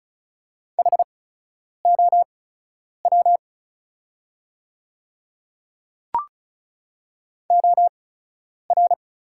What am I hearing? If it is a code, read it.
Morse code, heard as a keyed tone, about 34 wpm, spaced as for 10 wpm, 700 Hz. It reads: HOW OR
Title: 8x Character Spacing - Top 100 Words - 35wpm